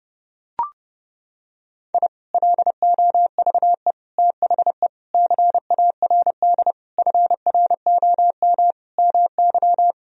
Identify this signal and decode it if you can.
Morse code, heard as a keyed tone, about 30 wpm, 700 Hz. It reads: I LOVE THE CARD FROM MY
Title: Sentences from Top 1000 Words - Review Part 4 - 30wpm